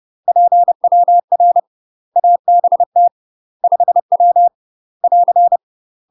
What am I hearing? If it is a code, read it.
Morse code, heard as a keyed tone, about 30 wpm, 700 Hz. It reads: PWR ABT 5W <AR>
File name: UK MASTER QSOs TRAINING / 30WPM / UK MASTER QSOs 0028